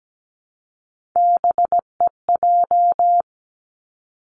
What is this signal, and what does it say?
Morse code, heard as a keyed tone, about 17 wpm, 700 Hz. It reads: BEJ
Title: Three Challenging Letter-Number Combinations 17wpm